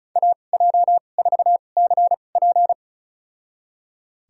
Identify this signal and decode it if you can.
Morse code, heard as a keyed tone, about 34 wpm, spaced as for 25 wpm, 700 Hz. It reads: AJ4CP
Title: USA Call Signs - 5 Characters - 2x Character Spacing - 35wpm